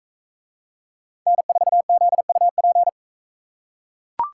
Morse code, heard as a keyed tone, about 41 wpm, 700 Hz. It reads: N4ZUP